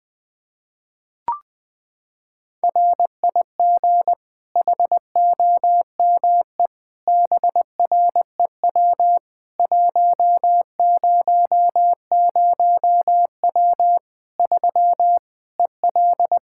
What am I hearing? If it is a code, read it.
Morse code, heard as a keyed tone, about 21 wpm, 700 Hz. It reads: RIG HOME BREW 100W 3 EL
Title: K1USN SST QSO Practice 20wpm